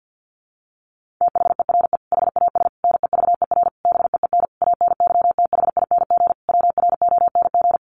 Morse code, heard as a keyed tone, about 50 wpm, 700 Hz. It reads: THERE HAS NEVER BEEN ANYTHING WRONG
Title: Sentences from Top 1500 Words 50wpm